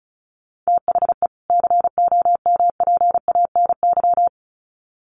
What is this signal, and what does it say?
Morse code, heard as a keyed tone, about 34 wpm, 700 Hz. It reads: THE COMPANY
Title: Sets of 2 Words 35wpm